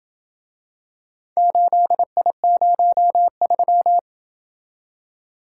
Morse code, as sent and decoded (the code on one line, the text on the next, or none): ---.. .. ----- ...--
8I03